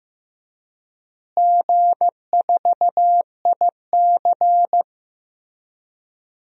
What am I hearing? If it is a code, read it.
Morse code, heard as a keyed tone, about 15 wpm, 700 Hz. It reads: G4IC
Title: DXCC-100 Call Signs - 4 Characters - Repeated 3x - 15wpm